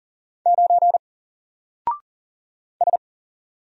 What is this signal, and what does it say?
Morse code, heard as a keyed tone, about 40 wpm, 700 Hz. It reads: 9 S
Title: Single Letter-Number 40wpm